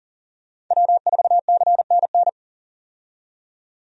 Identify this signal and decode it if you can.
Morse code, heard as a keyed tone, about 40 wpm, 700 Hz. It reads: W4CNN
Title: USA Call Signs - 5 Characters - 40wpm